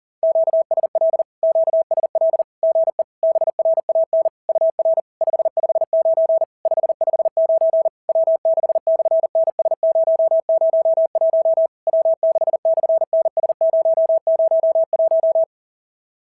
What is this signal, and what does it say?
Morse code, heard as a keyed tone, about 40 wpm, 650 Hz. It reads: QSL QSL GE BRAN UR 559 559 W6/NS001 W6/NS001
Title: SOTA QSO Practice 40wpm